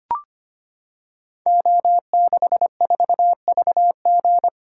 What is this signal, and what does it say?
Morse code, heard as a keyed tone, about 25 wpm, 700 Hz. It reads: O64VG
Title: Code Groups - 5 Letter:Numbers - 25wpm